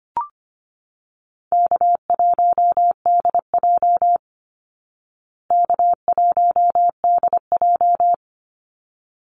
Morse code, heard as a keyed tone, about 25 wpm, 700 Hz. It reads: K1DJ K1DJ
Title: Call Signs Repeated 3x - 25wpm